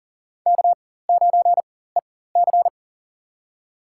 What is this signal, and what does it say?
Morse code, heard as a keyed tone, about 40 wpm, spaced as for 19 wpm, 700 Hz. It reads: K9EC
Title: USA Call Signs - 4 Characters - 4x Character Spacing - 40wpm